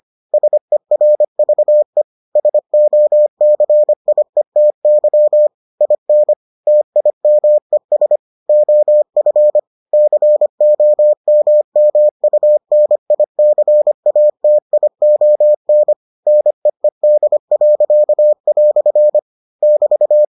Morse code, heard as a keyed tone, about 25 wpm, 600 Hz. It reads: SERVE SOCIETY IN TIMES OF COMMUNICATION NEED." <BT>